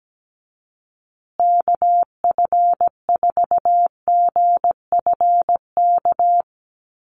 Morse code, heard as a keyed tone, about 17 wpm, 700 Hz. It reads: KF4GFK
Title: USA Call Signs Repeated 2x 17wpm